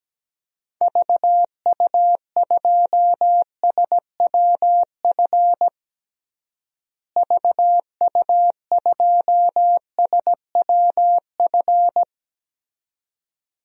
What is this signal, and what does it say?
Morse code, heard as a keyed tone, about 17 wpm, 700 Hz. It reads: VU2SWF VU2SWF
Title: DXCC-100 Call Signs - 6 Characters - Repeated 2x - 17wpm